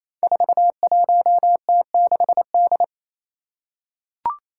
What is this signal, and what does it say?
Morse code, heard as a keyed tone, about 27 wpm, 700 Hz. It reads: 41T6D
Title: Five Challenging Letter-Number Combinations 28wpm